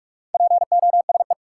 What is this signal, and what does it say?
Morse code, heard as a keyed tone, about 42 wpm, 700 Hz. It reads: POSE
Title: Sentences from Top 1000 Words - Review Part 2 - New Word Drill - 45wpm